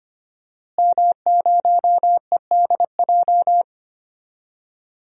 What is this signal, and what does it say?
Morse code, heard as a keyed tone, about 25 wpm, 700 Hz. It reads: M0EDJ
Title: DXCC-100 Call Signs - 5 Characters - Repeated 3x - 25wpm